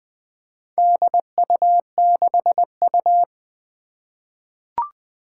Morse code, heard as a keyed tone, about 21 wpm, 700 Hz. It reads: DU6U
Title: Four Challenging Letter-Number Combinations 20wpm